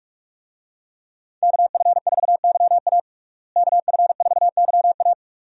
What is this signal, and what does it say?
Morse code, heard as a keyed tone, about 41 wpm, 700 Hz. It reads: KF4YA KF4YA
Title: USA Call Signs - 5 Characters - Repeated 3x - 45wpm